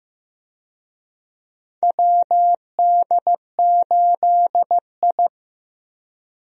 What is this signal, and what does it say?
Morse code, heard as a keyed tone, about 15 wpm, 700 Hz. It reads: WD8I
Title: USA Call Signs - 4 Characters - Repeated 2x - 15wpm